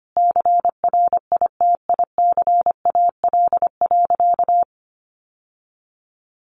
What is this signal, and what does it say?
Morse code, heard as a keyed tone, about 25 wpm, 700 Hz. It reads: CRITICAL.
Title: Sentences from Top 1100 Words 25wpm